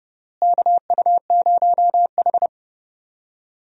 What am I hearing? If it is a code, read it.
Morse code, heard as a keyed tone, about 32 wpm, 700 Hz. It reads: KU0H
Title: USA Call Signs - 4 Characters - Repeated 3x - 30wpm